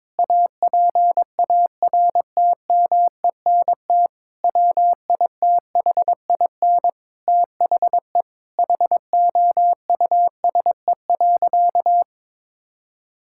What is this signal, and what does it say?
Morse code, heard as a keyed tone, about 22 wpm, 700 Hz. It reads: APARTMENT WITHIN THE HOUSE.
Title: Sentences from Top 1000 Words - Review Part 6 - 22wpm